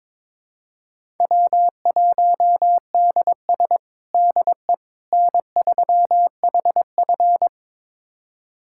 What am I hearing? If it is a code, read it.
Morse code, heard as a keyed tone, about 22 wpm, 700 Hz. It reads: W1DS DE N3HF
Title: USA Callsign DE Callsign - 4 Characters - 22wpm